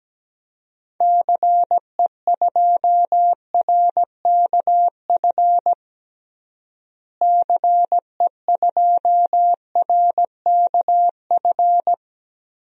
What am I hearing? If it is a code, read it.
Morse code, heard as a keyed tone, about 17 wpm, 700 Hz. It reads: CE2RKF CE2RKF
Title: DXCC-100 Call Signs - 6 Characters - Repeated 2x - 17wpm